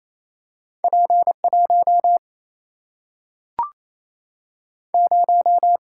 Morse code, heard as a keyed tone, about 28 wpm, 700 Hz. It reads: P 1 0
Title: Single-Letter Single-Number 28wpm